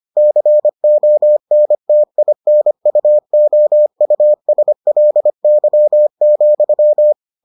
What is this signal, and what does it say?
Morse code, heard as a keyed tone, about 25 wpm, 600 Hz. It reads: CONTINUOUSLY,